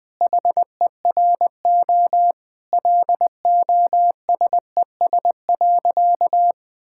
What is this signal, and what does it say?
Morse code, heard as a keyed tone, about 20 wpm, 700 Hz. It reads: HERO LOSES.
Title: Sentences with 6-Word Phrases - Spoken Beginning - 20wpm